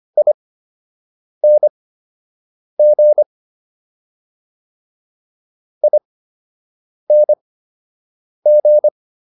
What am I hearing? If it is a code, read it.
Morse code, heard as a keyed tone, about 25 wpm, spaced as for 7 wpm, 600 Hz. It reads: ING ING